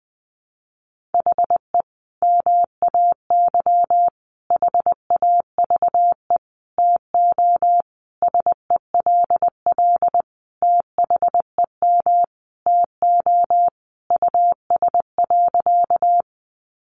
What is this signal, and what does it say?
Morse code, heard as a keyed tone, about 20 wpm, 700 Hz. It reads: HE MAY HAVE TO SELL THEM TO US.
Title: Sentences from Top 1100 Words 20wpm